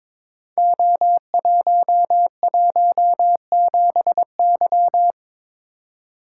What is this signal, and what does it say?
Morse code, heard as a keyed tone, about 22 wpm, 700 Hz. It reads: O117Y
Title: Code Groups - 5 Letter:Numbers - 22wpm